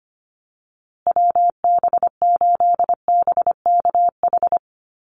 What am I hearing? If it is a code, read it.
Morse code, heard as a keyed tone, about 25 wpm, 700 Hz. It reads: WB8BKH
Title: USA Call Signs Repeated 3x 25wpm